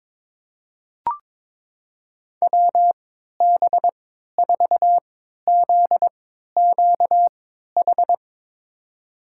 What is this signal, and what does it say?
Morse code, heard as a keyed tone, about 22 wpm, spaced as for 13 wpm, 700 Hz. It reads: WB4ZQH
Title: USA Call Signs - 3x Character Spacing - 22wpm